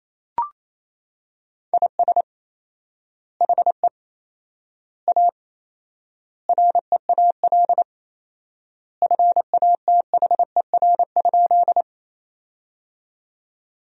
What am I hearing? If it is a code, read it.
Morse code, heard as a keyed tone, about 27 wpm, 700 Hz. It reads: IS HE A REAL FATHER?
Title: 4x Word-Spacing - Sentences from Top 300 Words - 28wpm